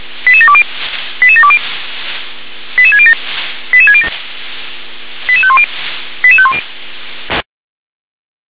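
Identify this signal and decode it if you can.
Five-tone selective call (ZVEI), heard as quick runs of five tones, tones of about 70 ms. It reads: 88720 88720 88607 88607 88510 88510
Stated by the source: ZVEI five-tone selective call 88148 received by SDR.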